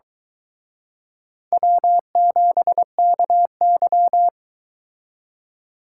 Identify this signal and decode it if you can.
Morse code, heard as a keyed tone, about 23 wpm, 700 Hz. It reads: W7KY